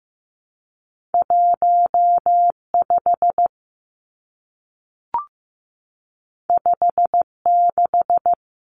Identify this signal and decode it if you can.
Morse code, heard as a keyed tone, about 15 wpm, 700 Hz. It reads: 15 56
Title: Numbers - 2-Digit 15wpm